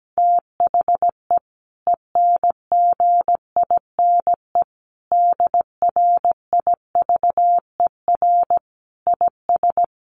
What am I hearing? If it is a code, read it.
Morse code, heard as a keyed tone, about 17 wpm, 700 Hz. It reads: THE ENGINE DRIVER IS
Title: Sentences from Top 1500 Words 17wpm